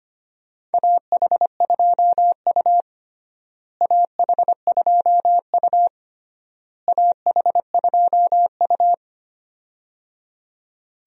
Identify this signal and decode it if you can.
Morse code, heard as a keyed tone, about 25 wpm, 700 Hz. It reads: AH2U AH2U AH2U
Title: USA Call Signs - 4 Characters - Repeated 3x - 25wpm